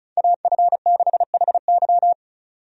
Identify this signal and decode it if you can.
Morse code, heard as a keyed tone, about 36 wpm, 700 Hz. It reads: AF6HY
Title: USA Call Signs - 5 Characters - 35wpm